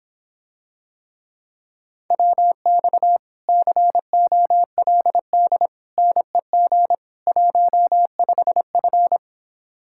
Morse code, heard as a keyed tone, about 26 wpm, 700 Hz. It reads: WX COLD NEG 15F